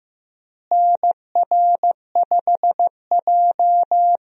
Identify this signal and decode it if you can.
Morse code, heard as a keyed tone, about 15 wpm, 700 Hz. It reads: NR5J
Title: USA Call Signs - 4 Characters - Repeated 2x - 15wpm